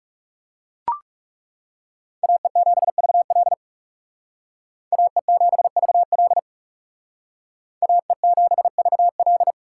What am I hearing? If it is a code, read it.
Morse code, heard as a keyed tone, about 39 wpm, 700 Hz. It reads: AE7VL AE7VL AE7VL